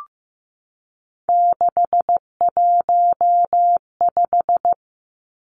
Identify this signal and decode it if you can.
Morse code, heard as a keyed tone, about 15 wpm, 700 Hz. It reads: 615
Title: Numbers - 3-Digit 15wpm